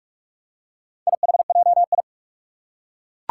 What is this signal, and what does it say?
Morse code, heard as a keyed tone, about 42 wpm, 700 Hz. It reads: IHJI
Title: Four Challenging Letter-Number Combinations 45wpm